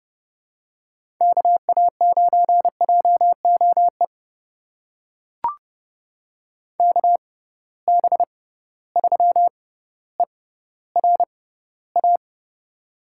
Morse code, heard as a keyed tone, about 32 wpm, spaced as for 16 wpm, 700 Hz. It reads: KA9JOE KB3ERA
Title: USA Call Signs - 6x Character Spacing - 30wpm